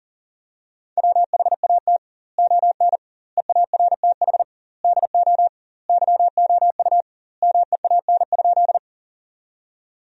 Morse code, heard as a keyed tone, about 40 wpm, 700 Hz. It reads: WHAT ON EARTH DO YOU MEAN?
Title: 2x Word-Spacing - Sentences from Top 200 Words - 40wpm